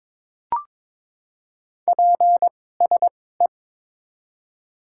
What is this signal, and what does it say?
Morse code, heard as a keyed tone, about 22 wpm, spaced as for 16 wpm, 700 Hz. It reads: PSE
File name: Top 90 QSO Elements - 2x Character Spacing - 22wpm